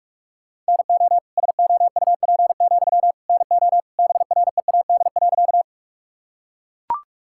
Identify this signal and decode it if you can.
Morse code, heard as a keyed tone, about 42 wpm, 700 Hz. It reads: NO SOUP, NO BREAD.
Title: Sentences from Top 1600 Words 45wpm